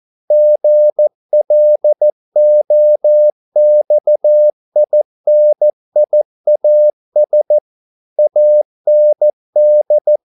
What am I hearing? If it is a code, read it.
Morse code, heard as a keyed tone, about 14 wpm, 600 Hz. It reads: GLOXINIAS AND
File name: mp3 files / a dogs mistake - 14wpm